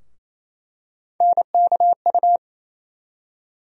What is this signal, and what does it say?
Morse code, heard as a keyed tone, about 28 wpm, 700 Hz. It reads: NKU
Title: USA Call Sign Postfixes - 3 Letters 28wpm